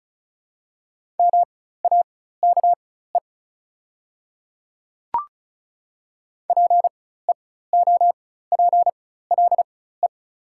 Morse code, heard as a keyed tone, about 34 wpm, spaced as for 16 wpm, 700 Hz. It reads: MAKE PEOPLE
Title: 4x Character Spacing - Top 200 Words - 35wpm